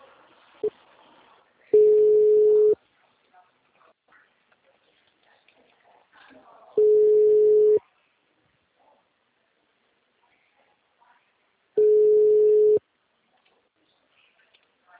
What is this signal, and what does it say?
Ringback tone (Europe, 425 Hz), with speech.